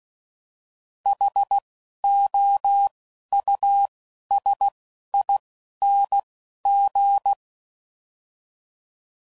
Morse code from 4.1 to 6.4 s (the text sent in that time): S I N